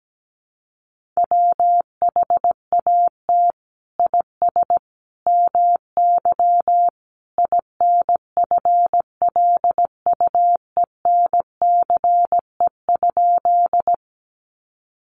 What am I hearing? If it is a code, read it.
Morse code, heard as a keyed tone, about 17 wpm, 700 Hz. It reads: WHAT IS MY INFLUENCE?
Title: Sentences from Top 1100 Words 17wpm